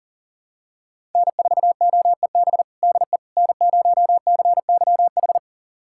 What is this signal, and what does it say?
Morse code, heard as a keyed tone, about 40 wpm, 700 Hz. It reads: N4OEB DE N0CYH
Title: USA Callsign DE Callsign 40wpm